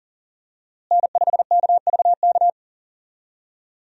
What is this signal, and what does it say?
Morse code, heard as a keyed tone, about 40 wpm, 700 Hz. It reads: N5KVK
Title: USA Call Signs - 5 Characters - Repeated 3x - 40wpm